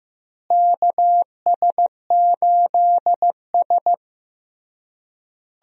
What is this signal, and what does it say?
Morse code, heard as a keyed tone, about 15 wpm, 700 Hz. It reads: KS8S
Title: USA Call Signs - 4 Characters - Repeated 3x - 15wpm